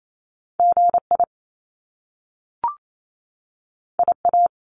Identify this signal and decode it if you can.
Morse code, heard as a keyed tone, about 28 wpm, 700 Hz. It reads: GI IA
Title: Top Two Letter Combinations in Words 28wpm